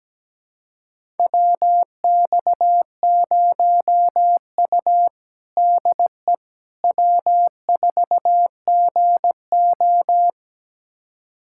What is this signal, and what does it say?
Morse code, heard as a keyed tone, about 17 wpm, 700 Hz. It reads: WX0U DE W4GO
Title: USA Callsign DE Callsign - 4 Characters - 17wpm